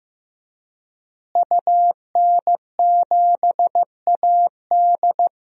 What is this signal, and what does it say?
Morse code, heard as a keyed tone, about 15 wpm, 700 Hz. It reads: UN7AD
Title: DXCC-100 Call Signs - 5 Characters - Repeated 3x - 15wpm